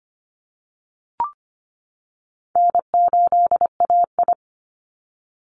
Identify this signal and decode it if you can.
Morse code, heard as a keyed tone, about 25 wpm, 700 Hz. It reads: N8AI
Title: Easy Call Signs - 25wpm